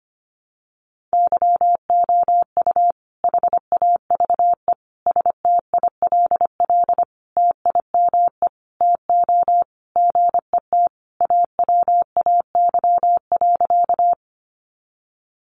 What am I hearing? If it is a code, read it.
Morse code, heard as a keyed tone, about 25 wpm, 700 Hz. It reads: YOU HAVE STILL TIME TO GET AWAY.
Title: Sentences from Top 600 Words 25wpm